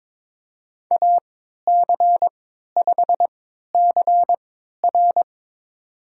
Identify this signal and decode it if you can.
Morse code, heard as a keyed tone, about 23 wpm, spaced as for 13 wpm, 700 Hz. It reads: AC5CR
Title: USA Call Signs - 5 Characters - 3x Character Spacing - 22wpm